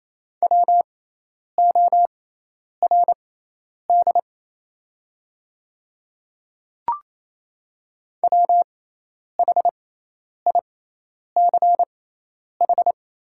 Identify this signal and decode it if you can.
Morse code, heard as a keyed tone, about 28 wpm, spaced as for 10 wpm, 700 Hz. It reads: WORD WHICH
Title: 6x Character Spacing - Top 200 Words - 28wpm